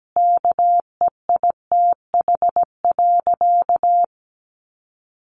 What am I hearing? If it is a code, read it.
Morse code, heard as a keyed tone, about 17 wpm, 700 Hz. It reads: KEITH.
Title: A Sound Of Thunder 17wpm